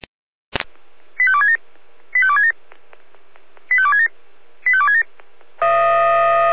Five-tone selective call (ZVEI), heard as quick runs of five tones, tones of about 70 ms. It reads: 85267 85267 85267 85267
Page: /SDR/ZVEI/